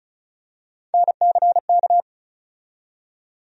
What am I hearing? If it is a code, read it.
Morse code, heard as a keyed tone, about 36 wpm, 700 Hz. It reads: NCK
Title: USA Call Sign Postfixes - 3 Letters 35wpm